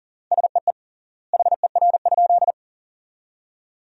Morse code, heard as a keyed tone, about 37 wpm, 700 Hz. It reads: SEE HER?
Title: 3x Word-Spacing - Sentences from Top 100 Words - 40wpm